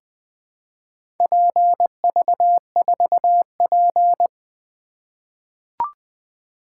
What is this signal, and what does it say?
Morse code, heard as a keyed tone, about 20 wpm, 700 Hz. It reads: PV4P
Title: Four Letter-Number Combinations - 20wpm